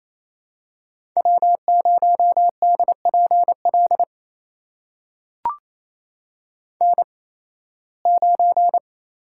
Morse code, heard as a keyed tone, about 28 wpm, spaced as for 18 wpm, 700 Hz. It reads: W0DPL N9
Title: USA Call Signs - 8x Character Spacing - 28wpm